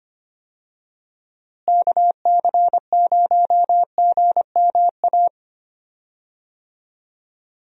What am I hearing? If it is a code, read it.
Morse code, heard as a keyed tone, about 25 wpm, 700 Hz. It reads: KC0GMA
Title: USA Call Signs - 6 Characters - Repeated 2x - 25wpm